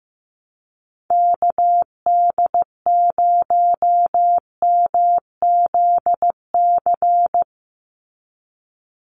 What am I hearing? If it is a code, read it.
Morse code, heard as a keyed tone, about 15 wpm, 700 Hz. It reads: KD0MZC
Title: USA Call Signs 15wpm